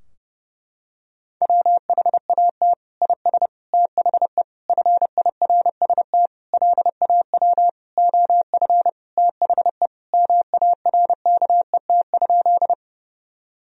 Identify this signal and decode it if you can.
Morse code, heard as a keyed tone, about 30 wpm, 700 Hz. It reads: WHAT IS THE FIRST LAW OF THE MARKET?
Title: Sentences from Top 1000 Words - Review Part 3 - 30wpm